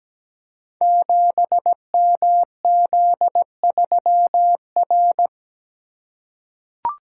Morse code, heard as a keyed tone, about 17 wpm, 700 Hz. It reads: 7MZ3R
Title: Code Groups - 5 Letter:Numbers - 17wpm